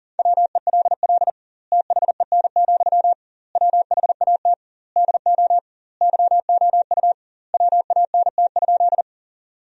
Morse code, heard as a keyed tone, about 37 wpm, 700 Hz. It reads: WELL THEN, WHAT DO YOU WANT?
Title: 2x Word-Spacing - Sentences from Top 200 Words - 40wpm